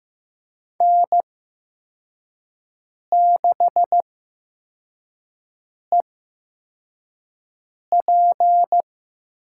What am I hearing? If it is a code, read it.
Morse code, heard as a keyed tone, about 15 wpm, spaced as for 4 wpm, 700 Hz. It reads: N6EP
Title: USA Call Signs - 8x Character Spacing - 15wpm